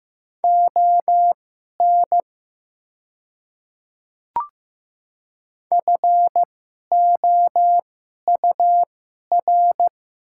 Morse code, heard as a keyed tone, about 16 wpm, spaced as for 11 wpm, 700 Hz. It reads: ON FOUR
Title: 2x Character Spacing - Top 300 Words - 15wpm